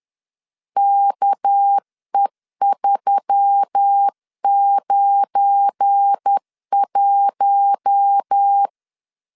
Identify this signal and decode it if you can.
Morse code, heard as a keyed tone, about 10 wpm, 800 Hz. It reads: KE391
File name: LEZIONI / 7 GF9Z / 7 GF9 10wpm 2747